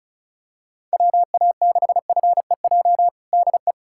Morse code, heard as a keyed tone, about 36 wpm, 700 Hz. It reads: WA6FEJ DE
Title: USA Callsign DE Callsign - 6 Characters - 35wpm